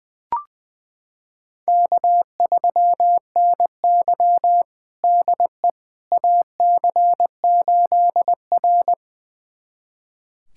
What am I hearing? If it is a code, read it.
Morse code, heard as a keyed tone, about 21 wpm, 700 Hz. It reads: K3NY DE AC8R
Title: USA Callsign DE Callsign - 4 Characters - 20wpm